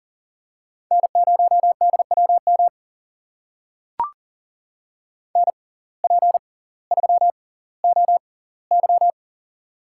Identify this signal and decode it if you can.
Morse code, heard as a keyed tone, about 40 wpm, spaced as for 20 wpm, 700 Hz. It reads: N0DWM NP3OY
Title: USA Call Signs - 5 Characters - 6x Character Spacing - 40wpm